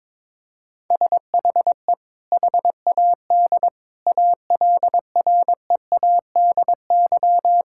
Morse code, heard as a keyed tone, about 21 wpm, 700 Hz. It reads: SHE HAD ALREADY
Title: Sentences from Top 1000 Words - Review Part 6 - 22wpm